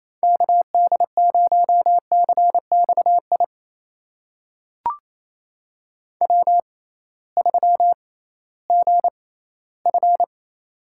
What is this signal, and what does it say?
Morse code, heard as a keyed tone, about 28 wpm, spaced as for 16 wpm, 700 Hz. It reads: KD0CXI W3GF
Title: USA Call Signs - 6x Character Spacing - 28wpm